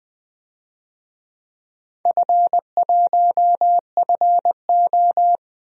Morse code, heard as a keyed tone, about 20 wpm, 700 Hz. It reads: F1FO
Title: DXCC-100 Call Signs - 4 Characters - 20wpm